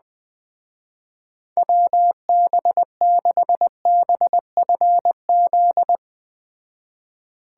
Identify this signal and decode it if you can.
Morse code, heard as a keyed tone, about 20 wpm, 700 Hz. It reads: WB6BFZ
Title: Speed Racing - USA 2x3 Call Signs 17wpm